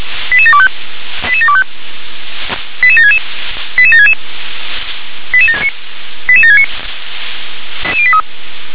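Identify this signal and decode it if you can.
Five-tone selective call (ZVEI), heard as quick runs of five tones, tones of about 70 ms. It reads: 88725 88725 88766 88766 88769
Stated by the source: ZVEI five-tone selective call 88756, received on SDR.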